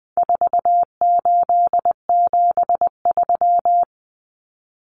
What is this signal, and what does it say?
Morse code, heard as a keyed tone, about 20 wpm, 700 Hz. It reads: 4873
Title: Common Cut Numbers - 4-Digit 20wpm